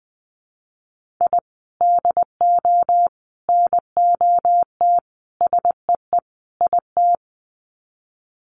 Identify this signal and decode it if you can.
Morse code, heard as a keyed tone, about 20 wpm, 700 Hz. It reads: I DO NOT SEE IT
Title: Sets of 5 Words Encore2 20wpm